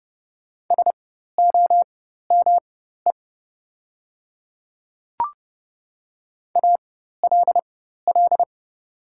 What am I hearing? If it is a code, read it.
Morse code, heard as a keyed tone, about 30 wpm, spaced as for 14 wpm, 700 Hz. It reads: SOME ALL
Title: 4x Character Spacing - Top 100 Words - 30wpm